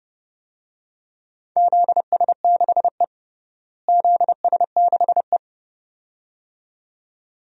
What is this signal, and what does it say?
Morse code, heard as a keyed tone, about 30 wpm, 700 Hz. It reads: ZS6E ZS6E
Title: DXCC-100 Call Signs - 4 Characters - Repeated 2x - 30wpm